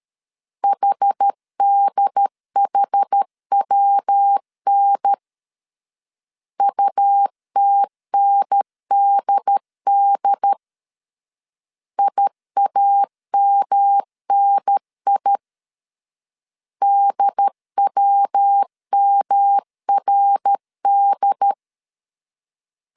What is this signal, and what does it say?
Morse code, heard as a keyed tone, about 13 wpm, 800 Hz. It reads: HDHWN UTNDD IAMNI DWMRD
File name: CW / DET / 1 3 1500C S B 12wpm 2828